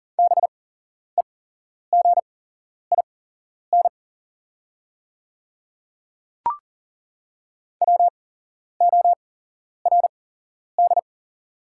Morse code, heard as a keyed tone, about 40 wpm, spaced as for 11 wpm, 690 Hz. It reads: BEGIN WORD
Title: 8x Character Spacing - Top 500 Words - 40wpm